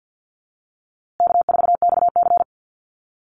Morse code, heard as a keyed tone, about 50 wpm, 700 Hz. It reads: K4XC
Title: USA Call Signs Repeated 3x 50wpm